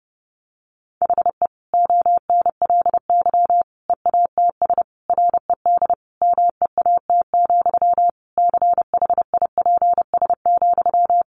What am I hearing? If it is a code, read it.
Morse code, heard as a keyed tone, about 28 wpm, 700 Hz. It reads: HE ONLY EATS RED MEAT, CHIPS,
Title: Easy Phrases 30wpm